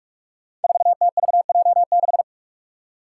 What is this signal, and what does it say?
Morse code, heard as a keyed tone, about 43 wpm, 700 Hz. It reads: 4TVJ6
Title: Five Challenging Letter-Number Combinations 45wpm